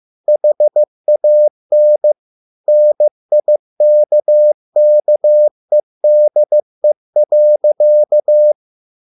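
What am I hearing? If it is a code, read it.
Morse code, heard as a keyed tone, about 15 wpm, 600 Hz. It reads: HAN NIKKEDE.
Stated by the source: Morse code at a slow 15 wpm from CW med Gnister.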